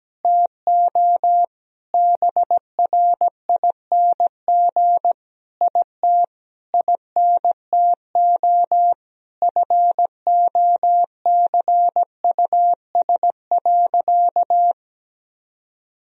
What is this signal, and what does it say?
Morse code, heard as a keyed tone, about 17 wpm, 700 Hz. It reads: TO BRING IT INTO FOCUS.
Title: Sentences from Top 1000 Words - Review Part 4 - 17wpm